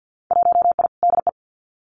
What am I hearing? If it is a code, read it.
Morse code, heard as a keyed tone, about 50 wpm, 700 Hz. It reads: 1I DE
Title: Winter Field Day Exchanges Encore 50wpm